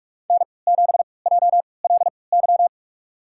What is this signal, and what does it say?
Morse code, heard as a keyed tone, about 45 wpm, spaced as for 25 wpm, 700 Hz. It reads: N7JLY
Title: USA Call Signs - 3x Character Spacing - 45wpm